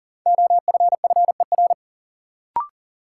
Morse code, heard as a keyed tone, about 40 wpm, 700 Hz. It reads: OFFER
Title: Sentences from Top 900 Words - New Word Drill - 40wpm